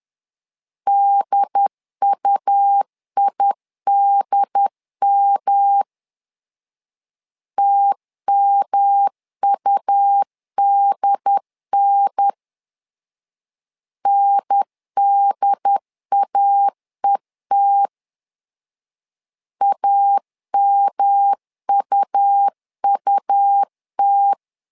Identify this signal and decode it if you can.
Morse code, heard as a keyed tone, about 11 wpm, 800 Hz. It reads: DUIDM TMUDN NDAET AMUUT
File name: LEZIONI / 3 MWRO / 3 M 10wpm 1828